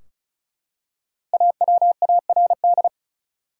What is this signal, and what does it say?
Morse code, heard as a keyed tone, about 36 wpm, 700 Hz. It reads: AWARD
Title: Five Letter Words - 35wpm